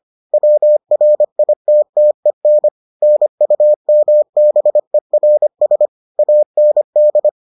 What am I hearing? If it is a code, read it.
Morse code, heard as a keyed tone, about 25 wpm, 600 Hz. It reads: WRITTEN NUMBERS AND